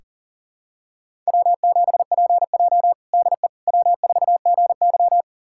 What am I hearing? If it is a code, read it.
Morse code, heard as a keyed tone, about 40 wpm, 700 Hz. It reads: W7PJ DE W4GY
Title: USA Callsign DE Callsign - 4 Characters - 40wpm